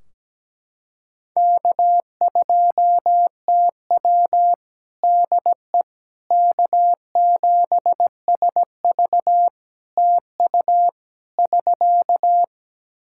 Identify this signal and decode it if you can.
Morse code, heard as a keyed tone, about 17 wpm, 700 Hz. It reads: K2TW DE K7SV TU <SK>
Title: K1USN SST QSO Practice 17wpm